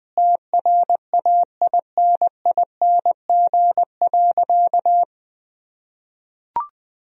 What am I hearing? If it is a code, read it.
Morse code, heard as a keyed tone, about 20 wpm, 700 Hz. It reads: TRAINING.
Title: Sentences from Top 1000 Words - Review Part 4 - 20wpm